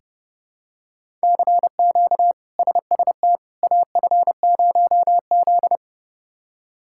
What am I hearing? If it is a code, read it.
Morse code, heard as a keyed tone, about 30 wpm, 700 Hz. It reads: CQ SST AF0Z
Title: K1USN SST QSO Practice - 30wpm